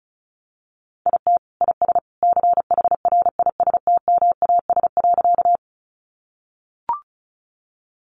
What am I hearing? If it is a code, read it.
Morse code, heard as a keyed tone, about 34 wpm, 700 Hz. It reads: IT IS CHRISTMAS.
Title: Easy Phrases 35wpm